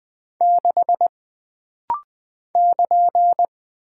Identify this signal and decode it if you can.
Morse code, heard as a keyed tone, about 20 wpm, 700 Hz. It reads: 6 <KN>
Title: Single Character (Letter:Number:Punctuation:Prosign) - Rapid-Fire - 20wpm